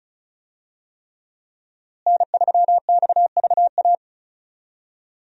Morse code, heard as a keyed tone, about 36 wpm, 700 Hz. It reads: N3XVA
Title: USA Call Signs - 5 Characters - Repeated 3x - 35wpm